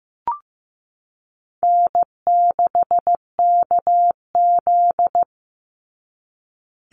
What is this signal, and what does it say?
Morse code, heard as a keyed tone, about 15 wpm, 700 Hz. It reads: N6KZ
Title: NA SOTA Chaser Callsigns 15wpm